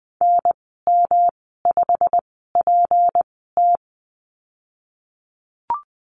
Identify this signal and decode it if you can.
Morse code, heard as a keyed tone, about 20 wpm, spaced as for 15 wpm, 700 Hz. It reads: NM5PT
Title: 231 Lesson 37g - Letter x - Call Signs 20wpm